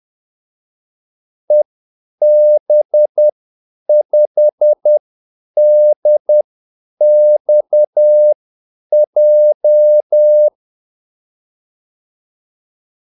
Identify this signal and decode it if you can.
Morse code, heard as a keyed tone, about 10 wpm, spaced as for 8 wpm, 600 Hz. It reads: EB5DXJ
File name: Callsigns-10-8wpm